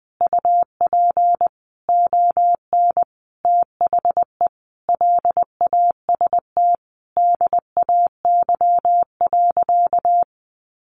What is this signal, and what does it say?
Morse code, heard as a keyed tone, about 20 wpm, 700 Hz. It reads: UP ON THE LAST DAY.
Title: Sentences from Top 1400 Words 20wpm